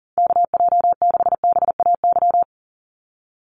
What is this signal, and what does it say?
Morse code, heard as a keyed tone, about 38 wpm, 690 Hz. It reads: KJ6BAY
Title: USA Call Signs 40wpm